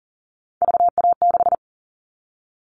Morse code, heard as a keyed tone, about 40 wpm, 700 Hz. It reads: VA6
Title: Callsign Prefixes for Top 25 DXCC - 40wpm